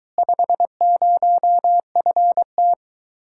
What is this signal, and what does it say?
Morse code, heard as a keyed tone, about 22 wpm, 700 Hz. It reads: 50FT